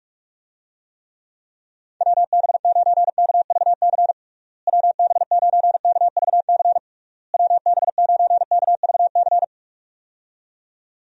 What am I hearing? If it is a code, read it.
Morse code, heard as a keyed tone, about 42 wpm, 700 Hz. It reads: WB9KVC WB9KVC WB9KVC
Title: USA Call Signs - 6 Characters - Repeated 3x - 45wpm